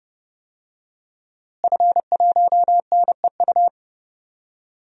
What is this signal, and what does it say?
Morse code, heard as a keyed tone, about 32 wpm, 700 Hz. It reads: F1NEU
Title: DXCC-100 Call Signs - 5 Characters - Repeated 2x - 30wpm